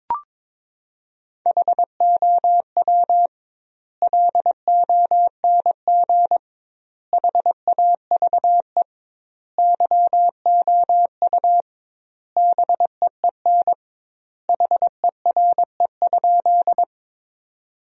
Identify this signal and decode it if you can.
Morse code, heard as a keyed tone, about 22 wpm, 700 Hz. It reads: HOW LONG HAVE YOU BEEN HERE?
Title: 2x Word-Spacing - Sentences from Top 200 Words - 22wpm